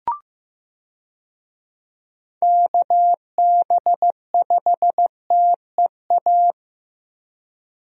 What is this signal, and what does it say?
Morse code, heard as a keyed tone, about 15 wpm, 700 Hz. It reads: KB5TEA
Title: USA Call Signs - 6 Characters - Repeated 3x - 15wpm